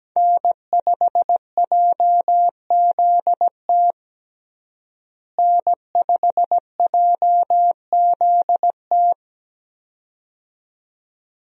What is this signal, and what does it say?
Morse code, heard as a keyed tone, about 17 wpm, 700 Hz. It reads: N5JZT N5JZT
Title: USA Call Signs - 5 Characters - Repeated 2x - 17wpm